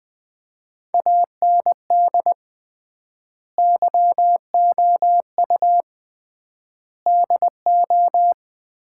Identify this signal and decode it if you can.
Morse code, heard as a keyed tone, about 20 wpm, 700 Hz. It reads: AND YOU DO
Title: 3x Word-Spacing - Sets of 4 Words - 20wpm